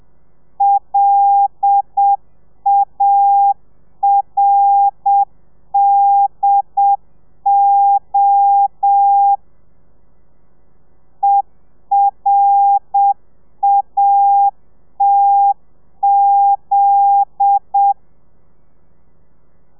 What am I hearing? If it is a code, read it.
Morse code, heard as a keyed tone, about 7 wpm, 800 Hz. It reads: LARDO ERATZ
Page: Lezione 1: audio a 30 CPM